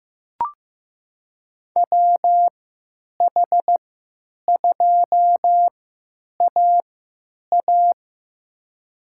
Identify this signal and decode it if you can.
Morse code, heard as a keyed tone, about 15 wpm, spaced as for 9 wpm, 700 Hz. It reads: WH2AA
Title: USA Call Signs - 5 Characters - 3x Character Spacing - 15wpm